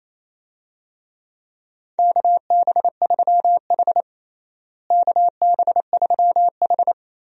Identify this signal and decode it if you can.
Morse code, heard as a keyed tone, about 27 wpm, 700 Hz. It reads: KB3H KB3H
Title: USA Call Signs - 4 Characters - Repeated 3x - 28wpm